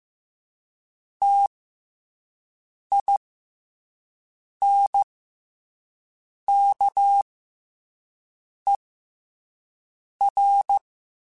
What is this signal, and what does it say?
Morse code, heard as a keyed tone, about 15 wpm, 800 Hz. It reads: T I N K E R
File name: Special INGs-mp3s / Special INGs-mp3s / SlowToFaster ing @10